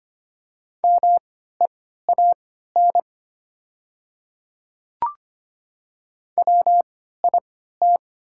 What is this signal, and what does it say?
Morse code, heard as a keyed tone, about 25 wpm, spaced as for 14 wpm, 700 Hz. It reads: MEAN WIT
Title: 3x Character Spacing - Top 300 Words - 25wpm